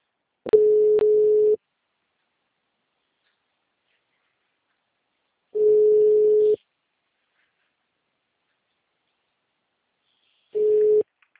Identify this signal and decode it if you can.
Ringback tone (Europe, 425 Hz).